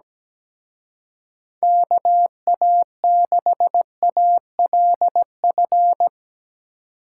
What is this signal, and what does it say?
Morse code, heard as a keyed tone, about 18 wpm, 700 Hz. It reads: KA6ALF
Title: Speed Racing - USA 2x3 Call Signs 17wpm